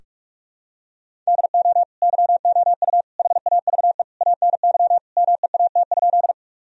Morse code, heard as a keyed tone, about 46 wpm, 700 Hz. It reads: DO YOU HAVE ANY MEAT?
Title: Sentences from Top 1000 Words - Review Part 3 - 45wpm